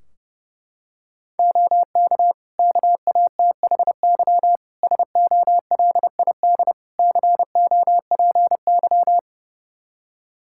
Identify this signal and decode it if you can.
Morse code, heard as a keyed tone, about 32 wpm, 700 Hz. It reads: OK KATHY SOLID COPY
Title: Practice QSO - Exchange 3 - More Info - 30wpm